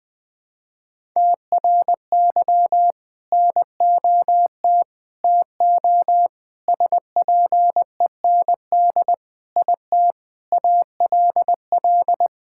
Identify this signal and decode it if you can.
Morse code, heard as a keyed tone, about 21 wpm, 700 Hz. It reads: TRY NOT TO SPEND IT ALL
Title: Sentences from Top 1000 Words - Review Part 5 - 20wpm